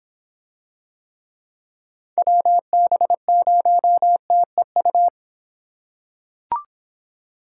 Morse code, heard as a keyed tone, about 25 wpm, 700 Hz. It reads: WB0TEU